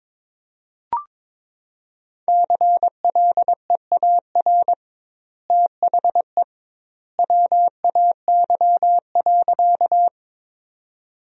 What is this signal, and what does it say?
Morse code, heard as a keyed tone, about 22 wpm, 700 Hz. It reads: CLEAR THE WAY.
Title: 2x Word-Spacing - Sentences from Top 500 Words - 22wpm